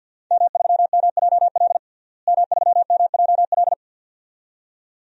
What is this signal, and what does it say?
Morse code, heard as a keyed tone, about 51 wpm, 700 Hz. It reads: M3MJL M3MJL
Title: DXCC-100 Call Signs - 5 Characters - Repeated 2x - 50wpm